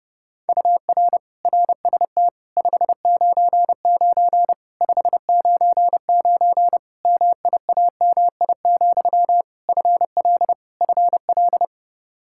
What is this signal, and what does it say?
Morse code, heard as a keyed tone, about 30 wpm, 700 Hz. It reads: UR RST 599 599 MIAMI, FL FL
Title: Practice QSO - Exchange 1 - Beginning - 30wpm